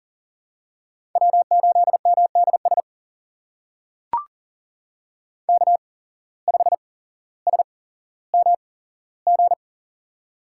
Morse code, heard as a keyed tone, about 38 wpm, spaced as for 17 wpm, 700 Hz. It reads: W8MDS K5SMG
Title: USA Call Signs - 5 Characters - 8x Character Spacing - 40wpm